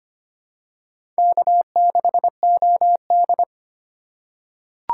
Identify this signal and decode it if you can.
Morse code, heard as a keyed tone, about 25 wpm, 700 Hz. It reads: K6OD
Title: USA Call Signs - 4 Characters - 2x Character Spacing - 25wpm